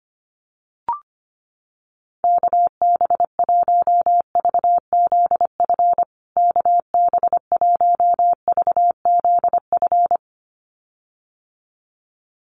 Morse code, heard as a keyed tone, about 25 wpm, 700 Hz. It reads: KB1VZF KB1VZF
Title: USA Call Signs Repeated 2x 25wpm